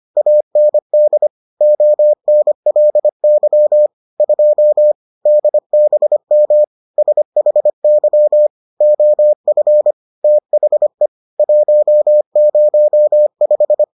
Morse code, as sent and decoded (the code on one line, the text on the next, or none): .- -. -.. / --- -. .-.. -.-- / ..--- / -.. -... -- / ... .... -.-- / --- ..-. / - .... / .---- ----- .....
AND ONLY 2 DBM SHY OF THE 105